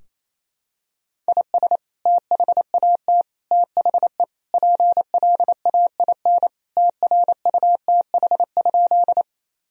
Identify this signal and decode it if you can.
Morse code, heard as a keyed tone, about 27 wpm, 700 Hz. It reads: IS THAT THE PLAIN TRUTH?
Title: Sentences from Top 1000 Words - Review Part 2 - 28wpm